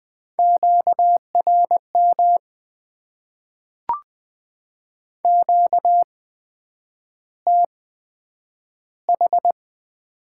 Morse code, heard as a keyed tone, about 20 wpm, spaced as for 9 wpm, 700 Hz. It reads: QRM QTH
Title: Top 90 QSO Elements - 8x Character Spacing - 20wpm